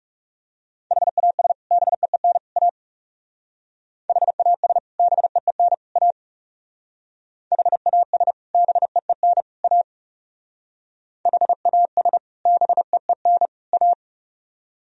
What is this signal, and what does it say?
Morse code, heard as a keyed tone, about 37 wpm, 700 Hz. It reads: HAS BEEN A HAS BEEN A HAS BEEN A HAS BEEN A
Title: Speed-Racing - Sets of 3 Words 30wpm